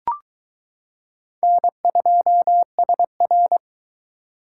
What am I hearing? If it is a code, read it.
Morse code, heard as a keyed tone, about 22 wpm, 700 Hz. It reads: N2SR